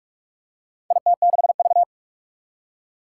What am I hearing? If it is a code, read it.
Morse code, heard as a keyed tone, about 46 wpm, 700 Hz. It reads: IT6V
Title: Four Challenging Letter-Number Combinations 45wpm